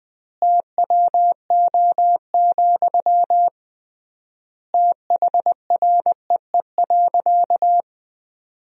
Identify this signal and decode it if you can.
Morse code, heard as a keyed tone, about 20 wpm, 700 Hz. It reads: TWO, THREE.
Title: 3x Word-Spacing - Sentences from Top 200 Words - 20wpm